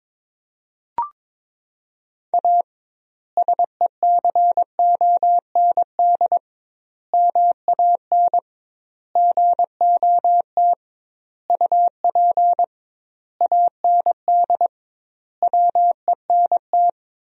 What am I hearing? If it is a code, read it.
Morse code, heard as a keyed tone, about 22 wpm, 700 Hz. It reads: A SECOND MAN GOT UP AND WENT
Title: 2x Word-Spacing - Sentences from Top 300 Words - 22wpm